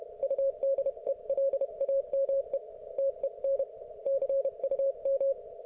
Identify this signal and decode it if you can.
Morse code, heard as a keyed tone, about 31 wpm, 530 Hz. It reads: UDELAME TEN CUM